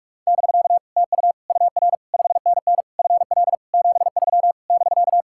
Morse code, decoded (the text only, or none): <BK> TU UR 5NN FL 73 <BK>